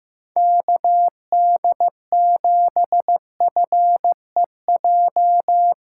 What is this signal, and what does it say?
Morse code, heard as a keyed tone, about 16 wpm, 700 Hz. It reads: KD7FEJ
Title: USA Callsign DE Callsign - 6 Characters - 15wpm